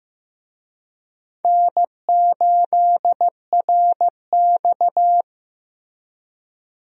Morse code, heard as a keyed tone, about 15 wpm, 700 Hz. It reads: N8RX
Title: USA Call Signs - 4 Characters - Repeated 3x - 15wpm